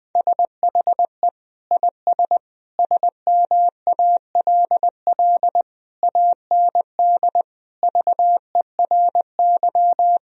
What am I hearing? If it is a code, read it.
Morse code, heard as a keyed tone, about 20 wpm, 700 Hz. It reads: SHE IS SMALL AND VERY
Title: Sentences from Top 1000 Words - Review Part 1 - 20wpm